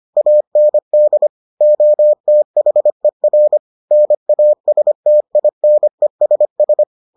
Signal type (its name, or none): Morse code